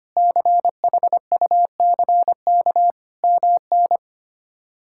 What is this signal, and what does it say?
Morse code, heard as a keyed tone, about 25 wpm, 700 Hz. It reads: CHUCK MN
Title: K1USN SST QSO Practice - 25wpm